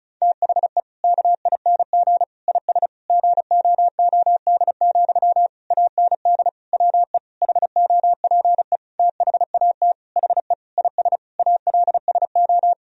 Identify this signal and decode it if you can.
Morse code, heard as a keyed tone, about 34 wpm, 700 Hz. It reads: THE KING IS GOOD, AND WE HOPE THAT HE IS ALSO
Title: Sentences from Top 1000 Words - Review Part 1 - 35wpm